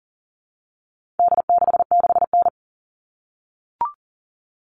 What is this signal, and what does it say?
Morse code, heard as a keyed tone, about 40 wpm, 700 Hz. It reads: D66N
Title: All Cut Numbers - 4-Digit 40wpm